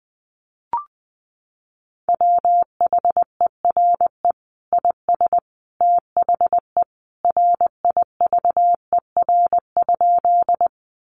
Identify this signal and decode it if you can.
Morse code, heard as a keyed tone, about 19 wpm, 700 Hz. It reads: WHERE IS THE RIVER?
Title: Sentences from Top 300 Words 20wpm